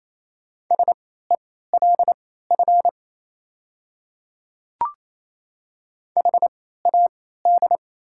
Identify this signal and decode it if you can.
Morse code, heard as a keyed tone, about 27 wpm, spaced as for 16 wpm, 700 Hz. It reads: SELF HAD
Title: 3x Character Spacing - Top 300 Words - 28wpm